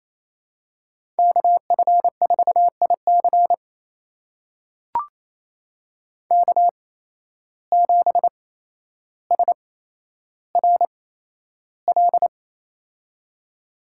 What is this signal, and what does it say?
Morse code, heard as a keyed tone, about 28 wpm, spaced as for 12 wpm, 700 Hz. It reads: KF4IC K7SRL
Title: USA Call Signs - 5 Characters - 8x Character Spacing - 28wpm